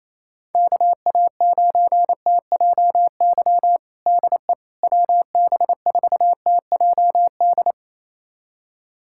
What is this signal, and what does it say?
Morse code, heard as a keyed tone, about 28 wpm, 700 Hz. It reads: KA9TJY DE WB4TJD
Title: USA Callsign DE Callsign 28wpm